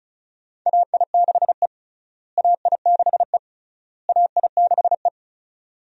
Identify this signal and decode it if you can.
Morse code, heard as a keyed tone, about 36 wpm, 700 Hz. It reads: AI6E AI6E AI6E
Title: USA Call Signs - 4 Characters - Repeated 3x - 35wpm